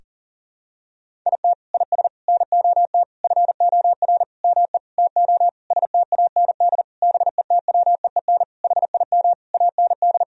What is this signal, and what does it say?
Morse code, heard as a keyed tone, about 40 wpm, 700 Hz. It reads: IT IS NOT FOR ME TO STAND BETWEEN HIM AND
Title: Sentences from Top 1000 Words - Review Part 1 - 40wpm